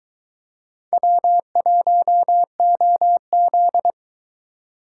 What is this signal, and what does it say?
Morse code, heard as a keyed tone, about 22 wpm, 700 Hz. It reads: W1OZ